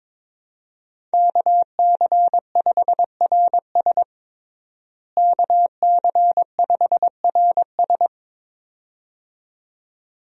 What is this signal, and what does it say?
Morse code, heard as a keyed tone, about 21 wpm, 700 Hz. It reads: KC5RS KC5RS
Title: USA Call Signs - 5 Characters - Repeated 2x - 22wpm